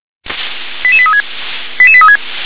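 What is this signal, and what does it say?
Five-tone selective call (ZVEI), heard as quick runs of five tones, tones of about 70 ms. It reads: 88836 88836
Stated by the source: Five-tone ZVEI selective call with code 88081, received on an SDR.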